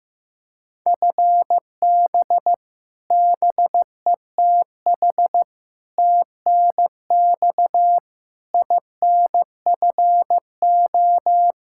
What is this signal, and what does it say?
Morse code, heard as a keyed tone, about 15 wpm, 700 Hz. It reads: FB BETH TNX INFO
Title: Practice QSO - Exchange 3 - More Info - 15wpm